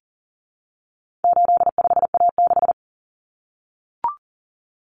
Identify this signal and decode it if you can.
Morse code, heard as a keyed tone, about 40 wpm, 700 Hz. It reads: 85A6
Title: Common Cut Numbers - 4-Digit 40wpm